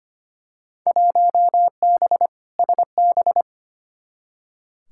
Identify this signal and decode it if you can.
Morse code, heard as a keyed tone, about 25 wpm, 700 Hz. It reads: 1B SB
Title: Field Day Exchanges - 25wpm